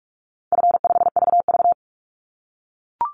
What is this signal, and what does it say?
Morse code, heard as a keyed tone, about 44 wpm, 700 Hz. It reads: F5VV
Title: Easy Call Signs - 45wpm